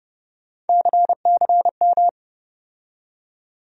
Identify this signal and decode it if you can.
Morse code, heard as a keyed tone, about 30 wpm, 700 Hz. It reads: CCM
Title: USA Call Sign Postfixes - 3 Letters 30wpm